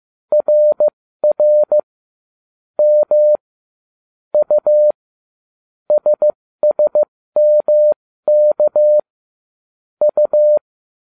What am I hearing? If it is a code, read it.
Morse code, heard as a keyed tone, about 15 wpm, spaced as for 13 wpm, 600 Hz. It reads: RR M U SSMK U